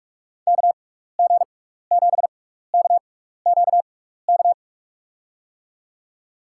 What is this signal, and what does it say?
Morse code, heard as a keyed tone, about 46 wpm, spaced as for 16 wpm, 700 Hz. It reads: KG7KQK
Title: USA Call Signs - 6x Character Spacing - 45wpm